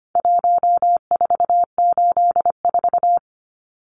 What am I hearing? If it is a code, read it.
Morse code, heard as a keyed tone, about 25 wpm, 700 Hz. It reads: 1484
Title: Numbers - Contest Style - 4-Digit 25wpm